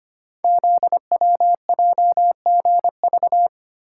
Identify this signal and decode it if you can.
Morse code, heard as a keyed tone, about 25 wpm, 700 Hz. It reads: ZWJGV
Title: Code Groups - 5 Letters - 25wpm